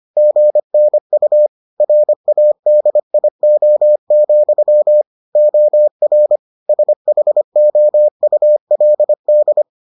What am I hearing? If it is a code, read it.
Morse code, heard as a keyed tone, about 25 wpm, 600 Hz. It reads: GNU RADIO, OR SHOULD